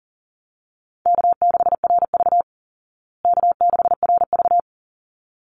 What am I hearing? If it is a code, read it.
Morse code, heard as a keyed tone, about 40 wpm, 700 Hz. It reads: K6RV K6RV
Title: Easy Call Signs Repeated 2x - 40wpm